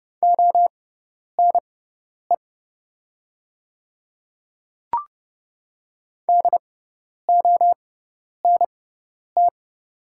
Morse code, heard as a keyed tone, about 30 wpm, spaced as for 10 wpm, 700 Hz. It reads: ONE DONT
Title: 6x Character Spacing - Top 300 Words - 30wpm